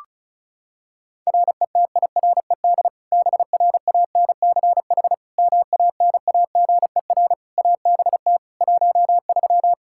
Morse code, heard as a keyed tone, about 36 wpm, 700 Hz. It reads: RETIRED BRANCH MANAGER ABT 13
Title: K1USN SST QSO Practice 35wpm